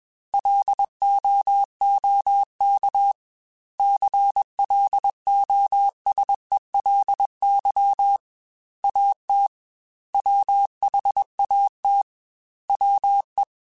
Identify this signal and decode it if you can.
Morse code, heard as a keyed tone, about 21 wpm, 800 Hz. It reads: LOOK CLOSELY AT WHAT WE
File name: Technical Subjects / Technical Subjects-mp3s / OR Gate @20